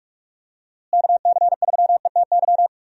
Morse code, heard as a keyed tone, about 46 wpm, 700 Hz. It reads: KC3ETY
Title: USA Call Signs - 6 Characters - Repeated 2x - 45wpm